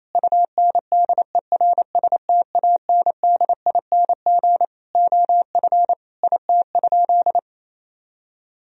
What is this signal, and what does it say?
Morse code, heard as a keyed tone, about 28 wpm, 700 Hz. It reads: UNDERSTANDING OF IT?
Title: Sentences from Top 1000 Words - Review Part 3 - 28wpm